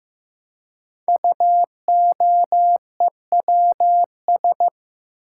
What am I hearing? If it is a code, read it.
Morse code, heard as a keyed tone, about 15 wpm, 700 Hz. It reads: UOEWS
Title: Code Groups - 5 Letters - 15wpm